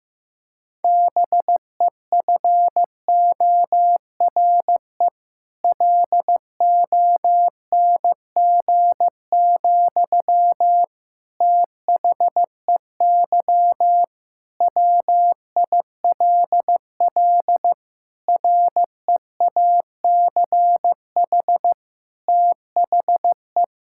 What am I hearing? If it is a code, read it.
Morse code, heard as a keyed tone, about 15 wpm, 700 Hz. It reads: BEFORE LONG, THEY WILL REACH THE
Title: Sentences from Top 1000 Words - Review Part 6 - 15wpm